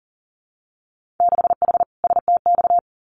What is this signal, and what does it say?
Morse code, heard as a keyed tone, about 39 wpm, 700 Hz. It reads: 6H STX
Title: Winter Field Day Exchanges Encore 40wpm